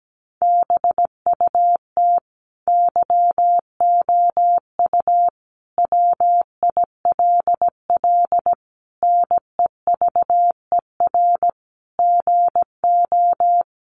Morse code, heard as a keyed tone, about 17 wpm, 700 Hz. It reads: BUT YOU WILL NEVER GO
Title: Sentences from Top 1500 Words 17wpm